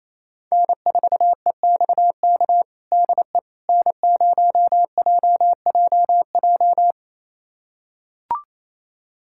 Morse code, heard as a keyed tone, about 28 wpm, 700 Hz. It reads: N4EXK DE N0JJJ
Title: USA Callsign DE Callsign - 5 Characters - 28wpm